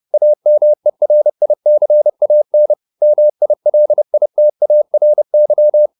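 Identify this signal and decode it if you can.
Morse code, heard as a keyed tone, about 30 wpm, 600 Hz. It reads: AMERICAN MILITARY
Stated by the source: Morse code at 29 wpm, sending Ken Follett's novel Never in English, from CW med Gnister.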